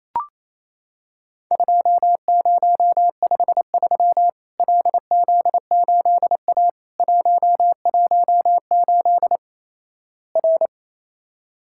Morse code, heard as a keyed tone, about 28 wpm, 700 Hz. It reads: 2053 LZ8A 118 R
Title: Worked All Europe DX Contest - QTC Exchange - 28wpm